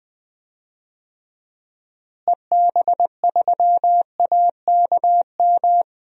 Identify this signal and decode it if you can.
Morse code, heard as a keyed tone, about 20 wpm, 700 Hz. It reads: EB3AKM
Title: DXCC-100 Call Signs - 6 Characters - Repeated 3x - 20wpm